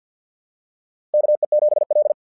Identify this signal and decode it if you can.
Morse code, heard as a keyed tone, about 50 wpm, 600 Hz. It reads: KE7L
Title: SOTA QSO Practice 50wpm